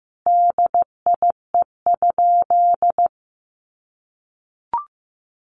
Morse code, heard as a keyed tone, about 15 wpm, 700 Hz. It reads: DIE?
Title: Sentences from Top 1300 Words 15wpm